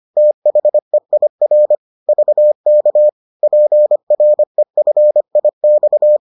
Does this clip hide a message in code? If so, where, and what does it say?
Yes: Morse code from 0.2 to 6.2 s, heard as a keyed tone, about 25 wpm, 600 Hz. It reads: THEIR VK PREFIX